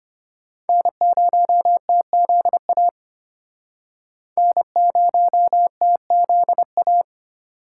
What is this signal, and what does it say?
Morse code, heard as a keyed tone, about 32 wpm, 700 Hz. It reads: N0TZA N0TZA